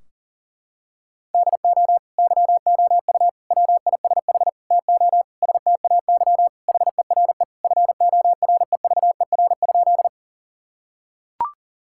Morse code, heard as a keyed tone, about 40 wpm, 700 Hz. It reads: DO YOU WISH TO STAY HERE FOREVER?
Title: Sentences from Top 1600 Words 40wpm